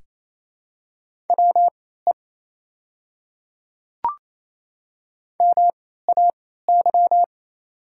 Morse code, heard as a keyed tone, about 27 wpm, spaced as for 16 wpm, 690 Hz. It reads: WE MAY
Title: 3x Character Spacing - Top 200 Words - 28wpm